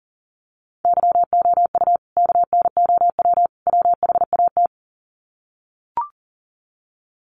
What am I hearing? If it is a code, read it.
Morse code, heard as a keyed tone, about 40 wpm, 700 Hz. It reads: YOU KNOW WHAT
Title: Sets of 3 Words 40wpm